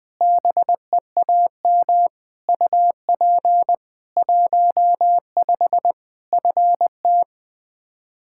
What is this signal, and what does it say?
Morse code, heard as a keyed tone, about 21 wpm, 700 Hz. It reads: BEAM UP 15 FT
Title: K1USN SST QSO Practice 20wpm